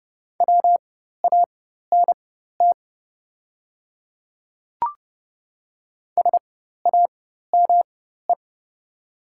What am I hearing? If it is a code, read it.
Morse code, heard as a keyed tone, about 30 wpm, spaced as for 14 wpm, 700 Hz. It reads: WANT SAME
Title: 4x Character Spacing - Top 200 Words - 30wpm